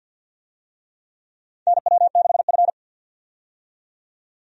Morse code, heard as a keyed tone, about 45 wpm, 700 Hz. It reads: NW6F
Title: USA Call Signs - 4 Characters - 50wpm